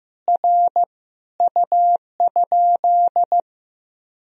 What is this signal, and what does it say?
Morse code, heard as a keyed tone, about 15 wpm, 700 Hz. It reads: R U?
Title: Sentences from Top 200 Words and Abbreviations 15wpm